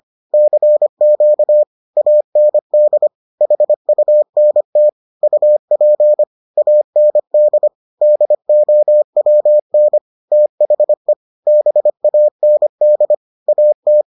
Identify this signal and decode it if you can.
Morse code, heard as a keyed tone, about 25 wpm, 600 Hz. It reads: CQ AND HUNT UP AND DOWN THE BAND AT